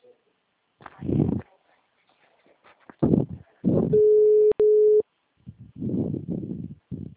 Ringback tone (Europe, 425 Hz).